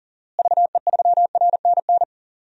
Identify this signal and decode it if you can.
Morse code, heard as a keyed tone, about 38 wpm, 700 Hz. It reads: VE3RNN
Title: DXCC-100 Call Signs - 6 Characters - Repeated 2x - 40wpm